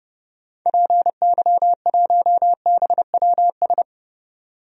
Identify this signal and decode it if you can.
Morse code, heard as a keyed tone, about 30 wpm, 700 Hz. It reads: PY1BWS
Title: DXCC-100 Call Signs - 6 Characters - 30wpm